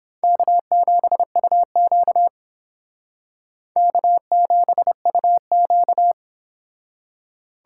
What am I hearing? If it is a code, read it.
Morse code, heard as a keyed tone, about 29 wpm, 700 Hz. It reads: K7UQ K7UQ